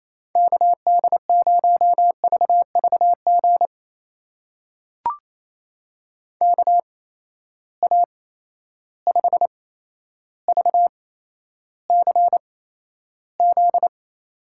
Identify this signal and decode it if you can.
Morse code, heard as a keyed tone, about 28 wpm, spaced as for 12 wpm, 700 Hz. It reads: KD0VVG KA5VCZ
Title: USA Call Signs - 6 Characters - 8x Character Spacing - 28wpm